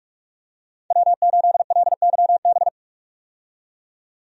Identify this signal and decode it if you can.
Morse code, heard as a keyed tone, about 48 wpm, 700 Hz. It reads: W8LYB
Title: USA Call Signs - 5 Characters - 45wpm